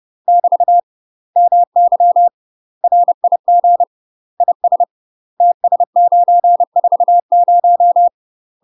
Morse code, heard as a keyed tone, about 30 wpm, 700 Hz. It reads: <BT> MY RIG IS TS940
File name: UK MASTER QSOs TRAINING / 30WPM / UK MASTER QSOs 0041